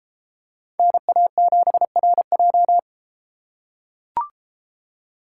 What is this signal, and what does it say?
Morse code, heard as a keyed tone, about 32 wpm, 700 Hz. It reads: NA7RJ